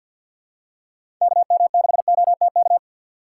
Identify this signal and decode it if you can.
Morse code, heard as a keyed tone, about 46 wpm, 700 Hz. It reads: KM6OTK
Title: USA Call Signs - 2x Character Spacing - 50wpm